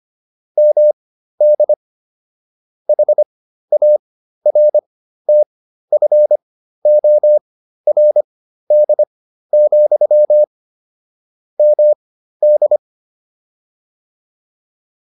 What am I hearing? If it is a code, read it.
Morse code, heard as a keyed tone, about 25 wpm, spaced as for 13 wpm, 600 Hz. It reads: MD HARTFORD, MD
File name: QSO401 13